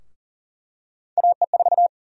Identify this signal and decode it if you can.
Morse code, heard as a keyed tone, about 40 wpm, 700 Hz. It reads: AE4
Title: USA Call Sign Prefixes - 40wpm